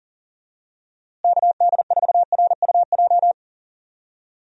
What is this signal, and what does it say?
Morse code, heard as a keyed tone, about 40 wpm, 700 Hz. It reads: KD4RUJ
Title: USA Call Signs - 6 Characters - Repeated 2x - 40wpm